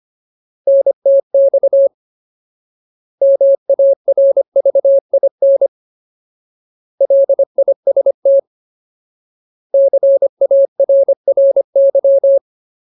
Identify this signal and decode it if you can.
Morse code, heard as a keyed tone, about 25 wpm, 550 Hz. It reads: NTX MARVIN LIST CARRY
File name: Ultimate Word List / 25WPM Ultimate